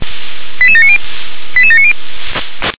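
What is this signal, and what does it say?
Five-tone selective call (ZVEI), heard as quick runs of five tones, tones of about 70 ms. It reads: 88790 88790
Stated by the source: Received via SDR; ZVEI five-tone selective call 88511.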